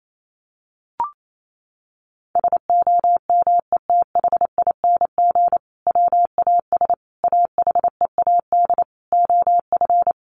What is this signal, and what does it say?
Morse code, heard as a keyed tone, about 28 wpm, 700 Hz. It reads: SOMETHING WAS AHEAD OF
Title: Sentences from Top 1100 Words 28wpm